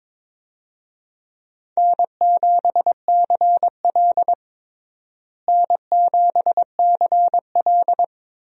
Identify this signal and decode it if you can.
Morse code, heard as a keyed tone, about 22 wpm, 700 Hz. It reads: N7CL N7CL
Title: USA Call Signs - 4 Characters - Repeated 3x - 22wpm